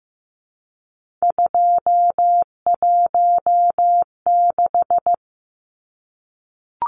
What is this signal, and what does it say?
Morse code, heard as a keyed tone, about 15 wpm, 700 Hz. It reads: 216
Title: Numbers - 3-Digit 15wpm